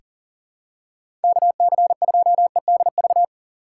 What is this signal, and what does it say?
Morse code, heard as a keyed tone, about 40 wpm, 700 Hz. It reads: KC2EDV
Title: USA Call Signs - 6 Characters - 3x Character Spacing - 40wpm